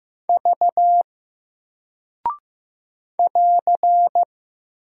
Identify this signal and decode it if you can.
Morse code, heard as a keyed tone, about 15 wpm, 700 Hz. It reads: V <AR>
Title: Single Character (Letter:Number:Punctuation:Prosign) 15wpm